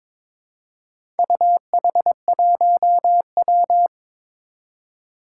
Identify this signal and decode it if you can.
Morse code, heard as a keyed tone, about 22 wpm, 700 Hz. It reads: UH1W
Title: Four Challenging Letter-Number Combinations 22wpm